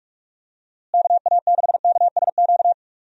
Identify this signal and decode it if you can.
Morse code, heard as a keyed tone, about 46 wpm, 700 Hz. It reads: KA6KSQ
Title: USA Call Signs - 6 Characters - Repeated 3x - 45wpm